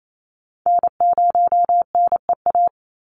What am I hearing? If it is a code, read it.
Morse code, heard as a keyed tone, about 28 wpm, 700 Hz. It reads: N0NEA
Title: USA Call Signs 28wpm